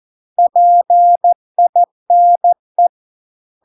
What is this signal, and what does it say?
Morse code, heard as a keyed tone, about 14 wpm, 700 Hz. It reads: PINE